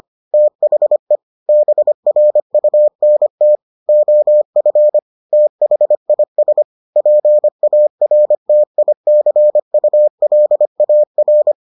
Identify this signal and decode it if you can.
Morse code, heard as a keyed tone, about 25 wpm, 600 Hz. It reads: THE BRUNT OF THIS PARTICULAR